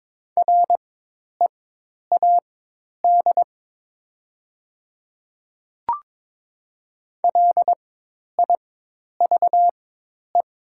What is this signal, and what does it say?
Morse code, heard as a keyed tone, about 22 wpm, spaced as for 10 wpm, 700 Hz. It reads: READ LIVE
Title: 4x Character Spacing - Top 300 Words - 22wpm